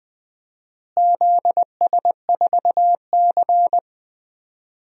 Morse code, heard as a keyed tone, about 20 wpm, 700 Hz. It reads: ZS4C
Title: DXCC-100 Call Signs - 4 Characters - Repeated 3x - 20wpm